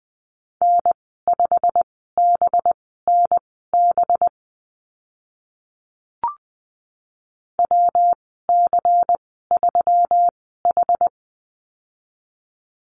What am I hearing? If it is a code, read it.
Morse code, heard as a keyed tone, about 20 wpm, spaced as for 15 wpm, 700 Hz. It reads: N5BNB WC3H
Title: 193 Lesson 31g - Letter v - Call Signs 20wpm